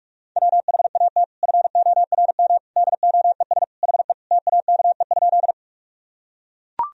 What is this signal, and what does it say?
Morse code, heard as a keyed tone, about 43 wpm, 700 Hz. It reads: WHAT FORM DOES HE TAKE?
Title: Sentences from Top 1000 Words - Review Part 1 - 45wpm